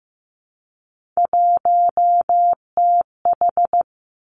Morse code, heard as a keyed tone, about 15 wpm, 700 Hz. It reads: 1TH
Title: Three Challenging Letter-Number Combinations 15wpm